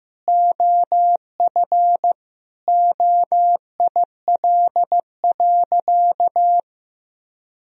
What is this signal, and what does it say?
Morse code, heard as a keyed tone, about 15 wpm, 700 Hz. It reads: OF OIL.
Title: Sentences from Top 1000 Words - Review Part 4 - 15wpm